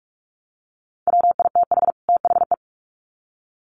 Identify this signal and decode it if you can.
Morse code, heard as a keyed tone, about 44 wpm, 700 Hz. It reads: WITH THE
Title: Sets of 2 Words 45wpm